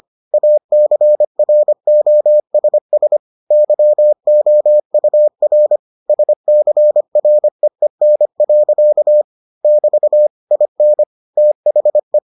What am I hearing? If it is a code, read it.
Morse code, heard as a keyed tone, about 25 wpm, 600 Hz. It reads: ACROSS YOUR SCREEN. <BT> IN THE